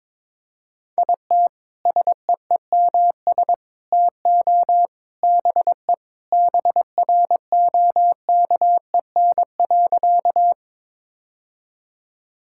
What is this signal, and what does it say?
Morse code, heard as a keyed tone, about 22 wpm, 700 Hz. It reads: IT SEEMS TO BE BROKEN.
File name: practice / 22wpm / Sentences with 3-Word Phrases - 22wpm